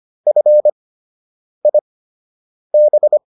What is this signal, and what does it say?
Morse code, heard as a keyed tone, about 25 wpm, spaced as for 8 wpm, 600 Hz. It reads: FIB